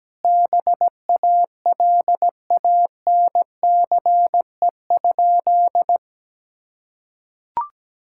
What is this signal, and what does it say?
Morse code, heard as a keyed tone, about 17 wpm, 700 Hz. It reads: BALANCE?
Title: Sentences from Top 1000 Words - Review Part 6 - 17wpm